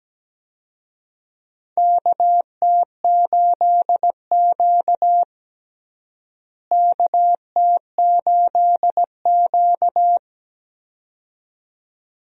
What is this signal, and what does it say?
Morse code, heard as a keyed tone, about 17 wpm, 700 Hz. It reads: KT8Q KT8Q
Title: USA Call Signs - 4 Characters - Repeated 2x - 17wpm